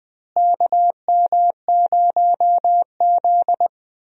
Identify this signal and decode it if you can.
Morse code, heard as a keyed tone, about 21 wpm, 700 Hz. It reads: KM0Z
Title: USA Call Signs - 4 Characters - Repeated 3x - 20wpm